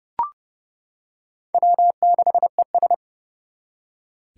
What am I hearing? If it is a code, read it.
Morse code, heard as a keyed tone, about 30 wpm, 700 Hz. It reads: W6ES
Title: Four Challenging Letter-Number Combinations 30wpm